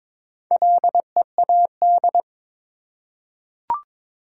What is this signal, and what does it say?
Morse code, heard as a keyed tone, about 22 wpm, 700 Hz. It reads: LEAD
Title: Four Letter Words - 22wpm